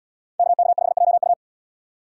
Morse code, heard as a keyed tone, about 69 wpm, 700 Hz. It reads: KK5JU